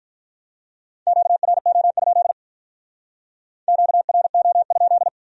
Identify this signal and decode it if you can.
Morse code, heard as a keyed tone, about 45 wpm, 700 Hz. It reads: QRO? QRO?